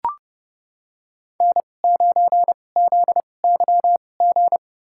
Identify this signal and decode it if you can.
Morse code, heard as a keyed tone, about 30 wpm, spaced as for 22 wpm, 700 Hz. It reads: N9ZYG
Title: USA Call Signs - 5 Characters - 2x Character Spacing - 30wpm